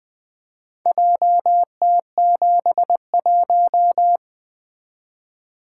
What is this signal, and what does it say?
Morse code, heard as a keyed tone, about 20 wpm, 700 Hz. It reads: JT71
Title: Four Letter-Number Combinations - 20wpm